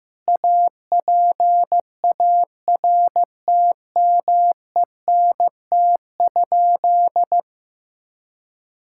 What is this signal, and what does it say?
Morse code, heard as a keyed tone, about 15 wpm, 700 Hz. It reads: APARTMENT?
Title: Sentences from Top 1000 Words - Review Part 6 - 15wpm